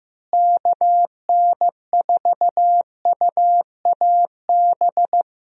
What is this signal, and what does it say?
Morse code, heard as a keyed tone, about 15 wpm, 700 Hz. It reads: KN4UAB
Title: USA Call Signs - 6 Characters - Repeated 2x - 15wpm